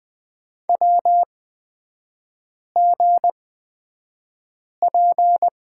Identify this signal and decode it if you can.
Morse code, heard as a keyed tone, about 20 wpm, spaced as for 5 wpm, 700 Hz. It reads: WGP
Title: 224 Lesson 36e - BT - Three Characters 20wpm